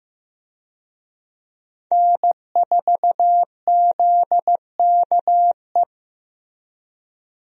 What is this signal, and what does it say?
Morse code, heard as a keyed tone, about 15 wpm, 700 Hz. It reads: N4ZKE
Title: USA Call Signs - 5 Characters - 15wpm